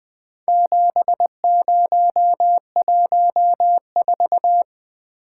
Morse code, heard as a keyed tone, about 20 wpm, 700 Hz. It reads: 7014
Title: Four Letter-Number Combinations - 20wpm